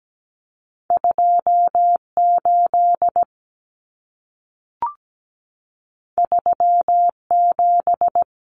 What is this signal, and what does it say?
Morse code, heard as a keyed tone, about 17 wpm, 700 Hz. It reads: 28 37
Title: Numbers - 2-Digit 17wpm